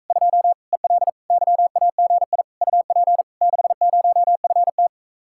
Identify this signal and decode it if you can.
Morse code, heard as a keyed tone, about 45 wpm, 700 Hz. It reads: 2 EL YAGI UP 60FT